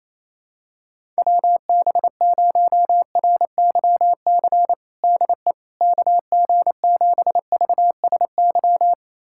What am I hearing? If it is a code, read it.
Morse code, heard as a keyed tone, about 28 wpm, 700 Hz. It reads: WB0RYC DE KG7VSY
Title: USA Callsign DE Callsign 28wpm